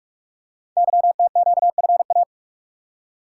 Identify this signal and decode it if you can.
Morse code, heard as a keyed tone, about 44 wpm, 700 Hz. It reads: YTQFA
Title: Code Groups - 5 Letters - 45wpm